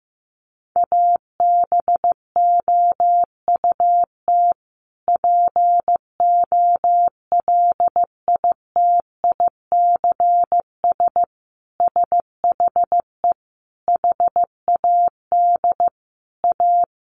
Morse code, heard as a keyed tone, about 15 wpm, 700 Hz. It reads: ABOUT POLITICS SHE HAD A
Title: Sentences from Top 1000 Words 15wpm